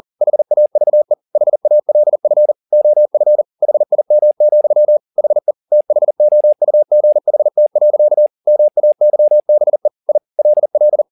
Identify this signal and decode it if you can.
Morse code, heard as a keyed tone, about 40 wpm, 600 Hz. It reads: HAVE HALF OF HIM, HE THOUGHT. MAYBE I LL
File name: the old man and the sea 2 40wpm